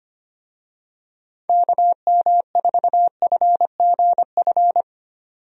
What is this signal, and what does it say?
Morse code, heard as a keyed tone, about 25 wpm, 700 Hz. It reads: KM4FGF
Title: USA Call Signs - 6 Characters - 25wpm